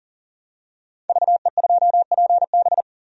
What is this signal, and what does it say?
Morse code, heard as a keyed tone, about 40 wpm, 700 Hz. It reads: VE2PB
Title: DXCC-100 Call Signs - 5 Characters - 40wpm